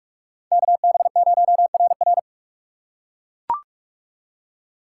Morse code, heard as a keyed tone, about 41 wpm, 700 Hz. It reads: KB0RR
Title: USA Call Signs - 2x Character Spacing - 45wpm